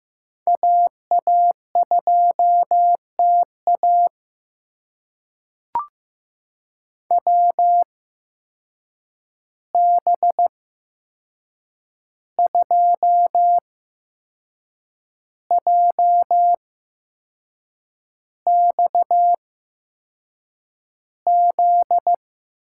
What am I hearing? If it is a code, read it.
Morse code, heard as a keyed tone, about 15 wpm, spaced as for 6 wpm, 700 Hz. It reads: AA2TA WB2JXZ
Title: USA Call Signs - 8x Character Spacing - 15wpm